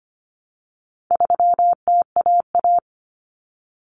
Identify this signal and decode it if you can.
Morse code, heard as a keyed tone, about 25 wpm, 700 Hz. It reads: 3TAA
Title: Common Cut Numbers - 4-Digit 25wpm